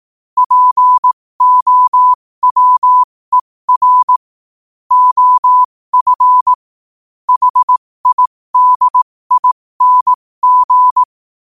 Morse code, heard as a keyed tone, about 18 wpm, spaced as for 15 wpm, 1000 Hz. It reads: POWER OF HIDING
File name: qotdfiles / 15wpm